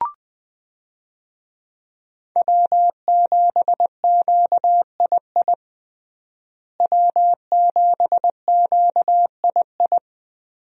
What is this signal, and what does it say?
Morse code, heard as a keyed tone, about 20 wpm, 700 Hz. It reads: W7QII W7QII
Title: USA Call Signs - 5 Characters - Repeated 2x - 20wpm